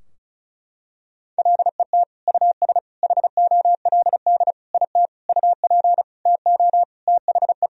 Morse code, heard as a keyed tone, about 36 wpm, 700 Hz. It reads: LET US HOLD IT UP TO THE
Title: Sentences from Top 1000 Words - Review Part 2 - 35wpm